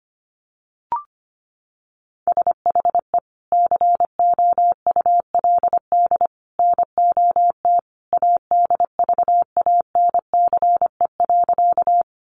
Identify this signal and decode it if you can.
Morse code, heard as a keyed tone, about 25 wpm, 700 Hz. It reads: SHE COULD NOT ADVANCE.
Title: Sentences from Top 1400 Words 25wpm